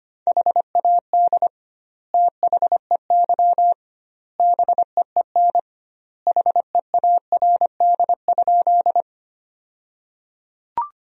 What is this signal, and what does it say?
Morse code, heard as a keyed tone, about 25 wpm, 700 Hz. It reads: HAD THEY BEEN HEARD?
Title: 2x Word-Spacing - Sentences from Top 400 Words - 25wpm